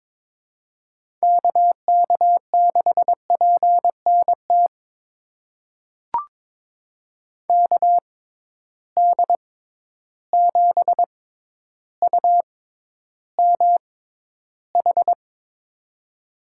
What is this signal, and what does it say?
Morse code, heard as a keyed tone, about 21 wpm, spaced as for 11 wpm, 700 Hz. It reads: KK6PNT KD7UMH
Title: USA Call Signs - 6 Characters - 6x Character Spacing - 22wpm